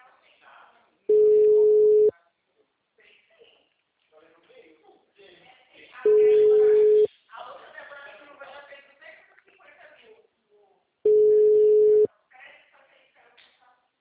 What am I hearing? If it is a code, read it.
Ringback tone (Europe, 425 Hz), with speech.
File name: out- -2019- - - .77384